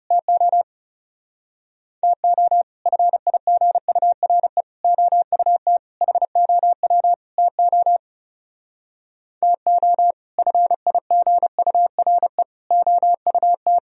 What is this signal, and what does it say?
Morse code, heard as a keyed tone, about 36 wpm, 700 Hz. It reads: TO TO FIGURE OUT HOW TO TO FIGURE OUT
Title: Speed-Racing - Sets of 5 Words 30wpm